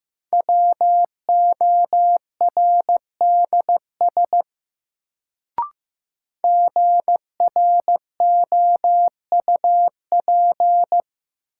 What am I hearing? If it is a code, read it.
Morse code, heard as a keyed tone, about 15 wpm, 700 Hz. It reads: WORDS GROUP
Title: Sets of 3 Words - New Word Drill - 15wpm